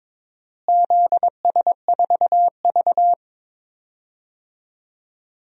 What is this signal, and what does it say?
Morse code, heard as a keyed tone, about 22 wpm, 700 Hz. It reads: ZS4V
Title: DXCC-100 Call Signs - 4 Characters - Repeated 3x - 22wpm